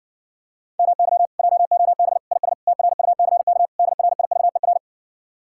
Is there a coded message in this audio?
Yes: Morse code from 0.8 to 4.8 s, heard as a keyed tone, about 53 wpm, 700 Hz. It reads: MY JOB IS TRUCK DRIVER